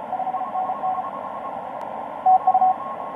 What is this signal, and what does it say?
Morse code, heard as a keyed tone, about 31 wpm, 750 Hz, with radio noise. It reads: 5NN TU
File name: tx5k dl4ny etc 12m cw 04mar2013 1521